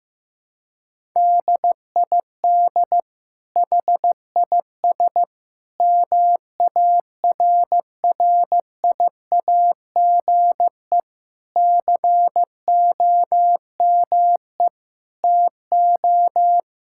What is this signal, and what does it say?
Morse code, heard as a keyed tone, about 15 wpm, 700 Hz. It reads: DID HIS MARRIAGE COME TO
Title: Sentences from Top 1000 Words - Review Part 5 - 15wpm